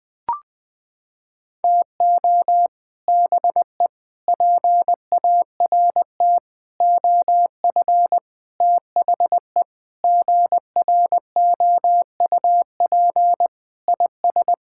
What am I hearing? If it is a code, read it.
Morse code, heard as a keyed tone, about 21 wpm, 700 Hz. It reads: TO BE PART OF THE GROUP IS
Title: Sentences with 4-Word Phrases - 20wpm